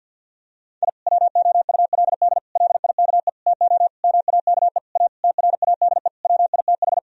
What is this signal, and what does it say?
Morse code, heard as a keyed tone, about 50 wpm, 700 Hz. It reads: I WOULD LIKE TO MAKE A TRADE WITH
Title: Sentences from Top 1000 Words - Review Part 4 - 50wpm